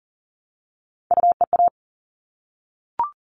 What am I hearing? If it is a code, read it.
Morse code, heard as a keyed tone, about 40 wpm, 700 Hz. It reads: UEA
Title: All Cut Numbers - 3-Digit 40wpm